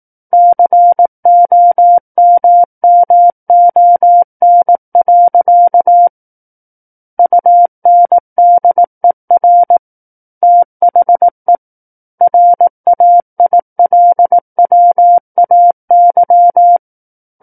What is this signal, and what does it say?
Morse code, heard as a keyed tone, about 18 wpm, 700 Hz. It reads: COMMON. UNDER THE RAILWAY
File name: 18 WPM